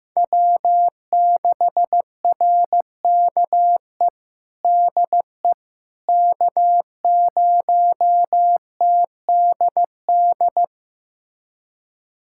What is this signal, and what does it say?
Morse code, heard as a keyed tone, about 16 wpm, 700 Hz. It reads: W6RKE DE K0TDD
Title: USA Callsign DE Callsign - 5 Characters - 15wpm